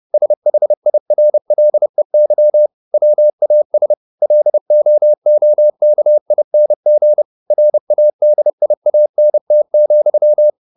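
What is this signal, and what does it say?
Morse code, heard as a keyed tone, about 30 wpm, 600 Hz. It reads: SHIRLEY WAS LOOKING RADIANT,
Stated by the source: Morse code at 30 wpm, the novel Never by Ken Follett in English, from CW med Gnister.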